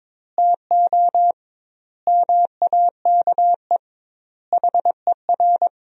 Morse code, heard as a keyed tone, about 22 wpm, 700 Hz. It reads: TO MAKE HER
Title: 2x Word-Spacing - Sentences from Top 400 Words - 22wpm